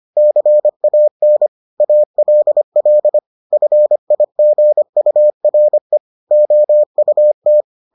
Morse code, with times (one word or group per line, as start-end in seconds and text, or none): 0.2-1.5 CAN
1.8-3.2 ALL
3.5-6.0 FIGURE
6.3-7.6 OUT